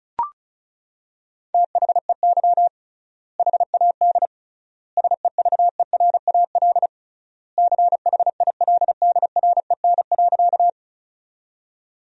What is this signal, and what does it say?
Morse code, heard as a keyed tone, about 36 wpm, 700 Hz. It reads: THEY HAD SEVERAL CHILDREN.
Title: 3x Word-Spacing - Sentences from Top 400 Words - 35wpm